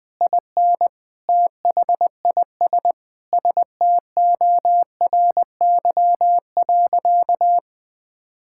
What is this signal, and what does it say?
Morse code, heard as a keyed tone, about 20 wpm, 700 Hz. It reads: IN THIS STORY.
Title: Sentences from Top 1600 Words 20wpm